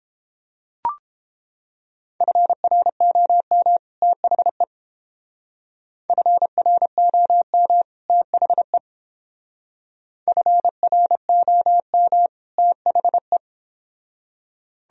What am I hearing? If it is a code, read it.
Morse code, heard as a keyed tone, about 29 wpm, 700 Hz. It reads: FROM THE FROM THE FROM THE